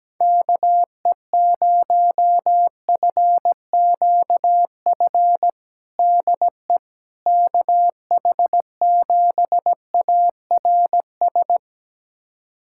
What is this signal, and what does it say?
Morse code, heard as a keyed tone, about 17 wpm, 700 Hz. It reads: KE0FQF DE KH7ARS
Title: USA Callsign DE Callsign - 6 Characters - 17wpm